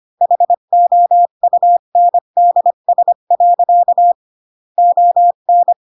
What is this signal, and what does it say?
Morse code, heard as a keyed tone, about 25 wpm, 700 Hz. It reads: HOUNDS. ON
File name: Pinocchio / Pinocchio ch4 20 25 ik1hge